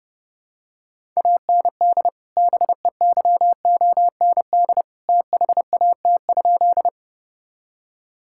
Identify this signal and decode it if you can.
Morse code, heard as a keyed tone, about 30 wpm, 700 Hz. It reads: AND BEYOND THAT?
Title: Sentences from Top 1000 Words - Review Part 6 - 30wpm